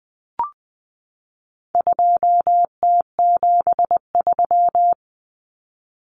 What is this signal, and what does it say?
Morse code, heard as a keyed tone, about 20 wpm, 700 Hz. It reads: 2T73
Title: Common Cut Numbers - 4-Digit 20wpm